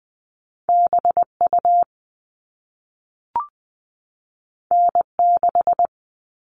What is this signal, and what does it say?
Morse code, heard as a keyed tone, about 20 wpm, 700 Hz. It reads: BU N6
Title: All Cut Numbers - 2-Digit 20wpm